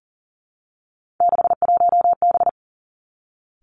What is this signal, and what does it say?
Morse code, heard as a keyed tone, about 38 wpm, 690 Hz. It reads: 61B
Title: Three Challenging Letter-Number Combinations 40wpm